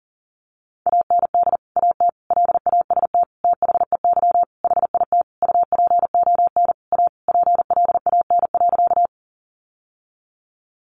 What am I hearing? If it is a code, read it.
Morse code, heard as a keyed tone, about 40 wpm, 700 Hz. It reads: AND AT LAST THEY HIT UPON A PLAN.
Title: Sentences from Top 1200 Words 40wpm